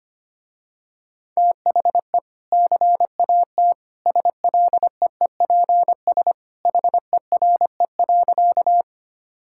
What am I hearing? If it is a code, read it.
Morse code, heard as a keyed tone, about 25 wpm, 700 Hz. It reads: THE CAT SLEEPS HERE.
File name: Easy Sentences with Hints - 25wpm